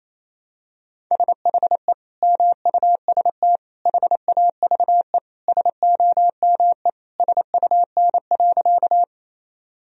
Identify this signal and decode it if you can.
Morse code, heard as a keyed tone, about 28 wpm, 700 Hz. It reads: SHE MUST HAVE SOME SUN.
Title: Sentences from Top 1000 Words - Review Part 1 - 28wpm